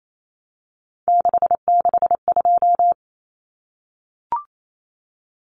Morse code, heard as a keyed tone, about 28 wpm, 700 Hz. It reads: 662
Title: Numbers - 3-Digit 28wpm